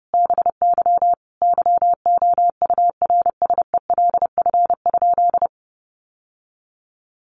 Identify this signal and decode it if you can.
Morse code, heard as a keyed tone, about 30 wpm, 700 Hz. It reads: BY YOURSELF?
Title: Sentences from Top 1100 Words 30wpm